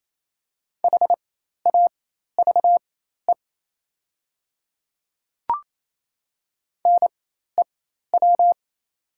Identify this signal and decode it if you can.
Morse code, heard as a keyed tone, about 27 wpm, spaced as for 13 wpm, 700 Hz. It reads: HAVE NEW
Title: 4x Character Spacing - Top 200 Words - 28wpm